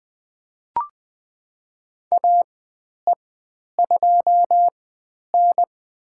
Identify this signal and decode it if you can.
Morse code, heard as a keyed tone, about 20 wpm, spaced as for 10 wpm, 700 Hz. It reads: AE2N
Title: 095 Lesson 16g - Letter c - Call Signs 20wpm